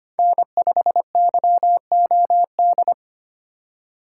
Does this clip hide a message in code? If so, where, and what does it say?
Yes: Morse code from 0.2 to 2.9 s, heard as a keyed tone, about 32 wpm, 700 Hz. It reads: N5YOD